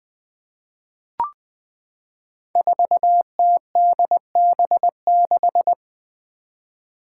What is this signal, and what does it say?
Morse code, heard as a keyed tone, about 20 wpm, 700 Hz. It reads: 4TDB6
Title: Five Challenging Letter-Number Combinations 20wpm